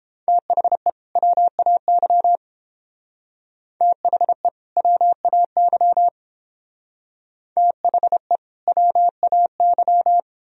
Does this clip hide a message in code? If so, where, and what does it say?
Yes: Morse code from 0.3 to 10.2 s, heard as a keyed tone, about 29 wpm, 700 Hz. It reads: THE WAY THE WAY THE WAY